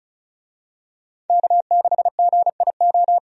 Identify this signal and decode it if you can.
Morse code, heard as a keyed tone, about 36 wpm, 700 Hz. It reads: K6GIO
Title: USA Call Signs - 5 Characters - 35wpm